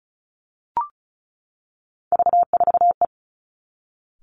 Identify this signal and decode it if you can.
Morse code, heard as a keyed tone, about 36 wpm, 700 Hz. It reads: V4E
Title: Three Challenging Letter-Number Combinations 35wpm